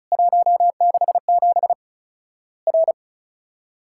Morse code, heard as a keyed tone, about 34 wpm, 700 Hz. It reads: 167 R
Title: Worked All Europe DX Contest - QTC Exchange - 35wpm